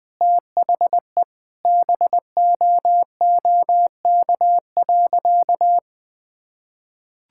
Morse code, heard as a keyed tone, about 20 wpm, 700 Hz. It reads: THE BOOK.
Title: Sentences from Top 1000 Words - Review Part 5 - 20wpm